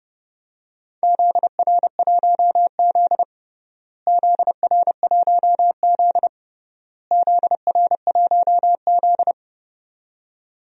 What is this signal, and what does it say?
Morse code, heard as a keyed tone, about 30 wpm, 700 Hz. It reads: ZR1Z ZR1Z ZR1Z
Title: DXCC-100 Call Signs - 4 Characters - Repeated 3x - 30wpm